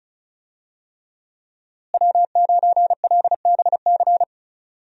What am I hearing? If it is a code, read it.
Morse code, heard as a keyed tone, about 34 wpm, 700 Hz. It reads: W9LBC
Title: USA Call Signs - 5 Characters - Repeated 3x - 35wpm